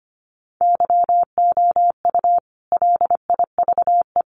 Morse code, heard as a keyed tone, about 25 wpm, 700 Hz. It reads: YOU LIVE
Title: Sentences from Top 1000 Words 25wpm